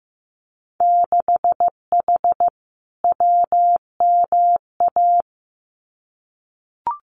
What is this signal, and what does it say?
Morse code, heard as a keyed tone, about 15 wpm, 700 Hz. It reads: 6H WMA
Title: Winter Field Day Exchanges Encore 15wpm